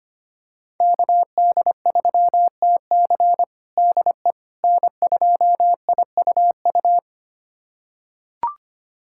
Morse code, heard as a keyed tone, about 25 wpm, 700 Hz. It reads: KD3TC DE N2IUU
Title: USA Callsign DE Callsign - 5 Characters - 25wpm